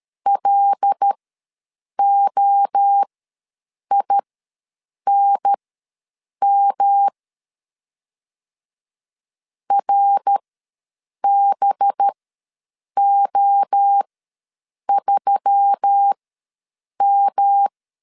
Morse code, heard as a keyed tone, about 13 wpm, spaced as for 7 wpm, 800 Hz. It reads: LOINM RBO3M
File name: LEZIONI / 5 LBVC / 5 LB ws 12wpm 3034